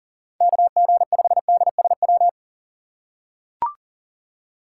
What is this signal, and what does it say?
Morse code, heard as a keyed tone, about 41 wpm, 700 Hz. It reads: KG5DSW
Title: USA Call Signs - 6 Characters - 8x Character Spacing - 40wpm